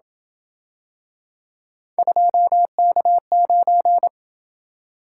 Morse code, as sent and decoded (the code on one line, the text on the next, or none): ..--- -.- ----.
2K9